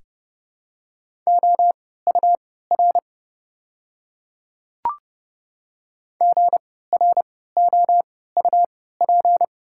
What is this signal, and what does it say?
Morse code, heard as a keyed tone, about 29 wpm, spaced as for 17 wpm, 700 Hz. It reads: OUR GROUP
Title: 3x Character Spacing - Top 300 Words - 30wpm